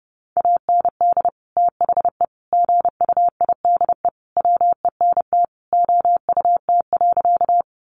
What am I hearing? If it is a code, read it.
Morse code, heard as a keyed tone, about 30 wpm, 700 Hz. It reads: AND THE GUIDE WENT OUT.
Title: Sentences from Top 600 Words 30wpm